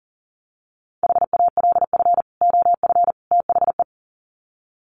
Morse code, heard as a keyed tone, about 40 wpm, 700 Hz. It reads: HALF OF THE
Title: Sets of 3 Words 40wpm